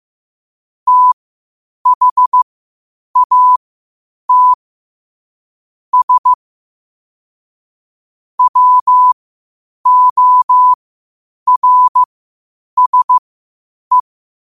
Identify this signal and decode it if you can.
Morse code, heard as a keyed tone, about 15 wpm, spaced as for 8 wpm, 1000 Hz. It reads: THAT S WORSE
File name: qotdfiles / 8wpm